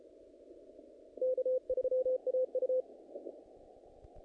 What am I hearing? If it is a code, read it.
Morse code, heard as a keyed tone, about 33 wpm, 510 Hz. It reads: K3AU
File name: contests / naqp-cw-k5tr-op-k5pi- -2000